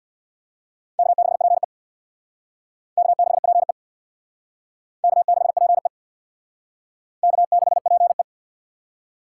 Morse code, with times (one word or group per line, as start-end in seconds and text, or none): 1.0-1.6 K6PE
3.0-3.7 K6PE
5.0-5.9 K6PE
7.2-8.2 K6PE